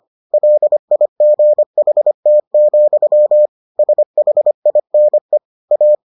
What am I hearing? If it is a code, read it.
Morse code, heard as a keyed tone, about 25 wpm, 600 Hz. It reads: LIGHT, SHINE A